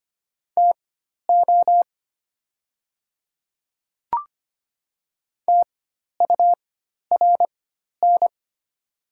Morse code, heard as a keyed tone, about 25 wpm, spaced as for 12 wpm, 690 Hz. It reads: TO TURN
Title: 4x Character Spacing - Top 200 Words - 25wpm